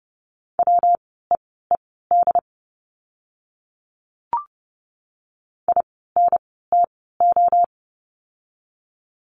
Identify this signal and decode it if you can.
Morse code, heard as a keyed tone, about 30 wpm, spaced as for 17 wpm, 700 Hz. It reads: WEED INTO
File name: course / 30:10wpm / 115 Lesson 19f - Letter w - Words 30wpm